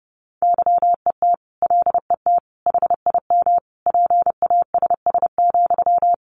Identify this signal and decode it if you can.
Morse code, heard as a keyed tone, about 30 wpm, 700 Hz. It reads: YET LET HIM PASS,
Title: Sentences from Top 1500 Words 30wpm